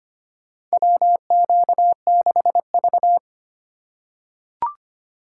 Morse code, heard as a keyed tone, about 25 wpm, 700 Hz. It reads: WQ6V
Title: USA Call Signs - 4x Character Spacing - 25wpm